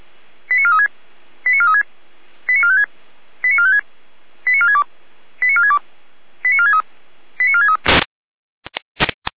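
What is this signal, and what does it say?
Five-tone selective call (ZVEI), heard as quick runs of five tones, tones of about 80 ms. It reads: 89436 89436 89456 89456 89462 89462 89463 89463
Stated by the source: ZVEI five-tone call 89417 received on SDR.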